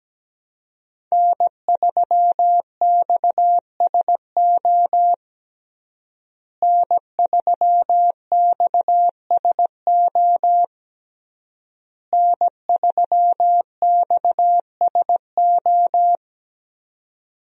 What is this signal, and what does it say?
Morse code, heard as a keyed tone, about 17 wpm, 700 Hz. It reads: N3XSO N3XSO N3XSO
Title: USA Call Signs - 5 Characters - Repeated 3x - 17wpm